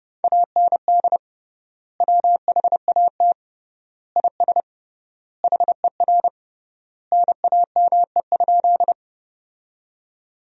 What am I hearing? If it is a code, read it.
Morse code, heard as a keyed tone, about 30 wpm, 700 Hz. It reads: AND WHAT IS HER NAME?
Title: 3x Word-Spacing - Sentences from Top 200 Words - 30wpm